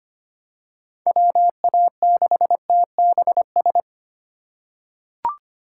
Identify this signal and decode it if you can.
Morse code, heard as a keyed tone, about 25 wpm, 700 Hz. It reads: WA6TBS
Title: USA Call Signs - 6 Characters - 3x Character Spacing - 25wpm